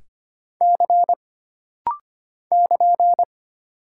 Morse code, heard as a keyed tone, about 25 wpm, 700 Hz. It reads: C <KN>
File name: Single Character (Letter:Number:Punctuation:Prosign) - Rapid-Fire - 25wpm